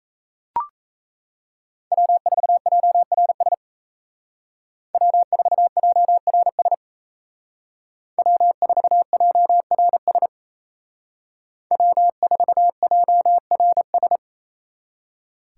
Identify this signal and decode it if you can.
Morse code, heard as a keyed tone, about 36 wpm, 700 Hz. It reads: W4JRS W4JRS W4JRS W4JRS